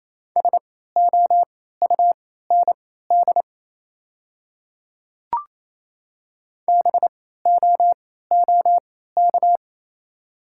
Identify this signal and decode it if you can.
Morse code, heard as a keyed tone, about 28 wpm, spaced as for 16 wpm, 700 Hz. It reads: SOUND BOOK
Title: 3x Character Spacing - Top 500 Words - 28wpm